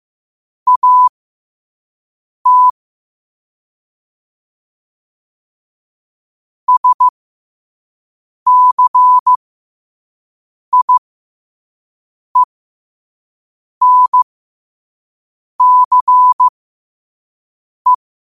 Morse code, heard as a keyed tone, about 15 wpm, spaced as for 5 wpm, 1000 Hz. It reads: AT SCIENCE
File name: qotdfiles / 5wpm